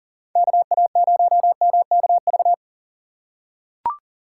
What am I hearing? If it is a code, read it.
Morse code, heard as a keyed tone, about 37 wpm, 700 Hz. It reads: KA0MKV
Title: USA Call Signs - 6 Characters - 4x Character Spacing - 40wpm